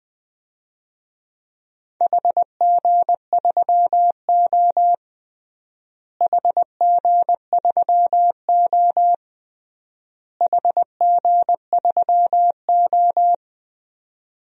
Morse code, heard as a keyed tone, about 20 wpm, 700 Hz. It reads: HG3O HG3O HG3O
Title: DXCC-100 Call Signs - 4 Characters - Repeated 3x - 20wpm